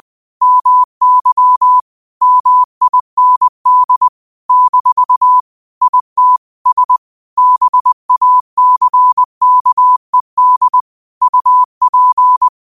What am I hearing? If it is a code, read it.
Morse code, heard as a keyed tone, about 20 wpm, 1000 Hz. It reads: MY MIND - IT S BACKED UP
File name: qotdfiles / 20wpm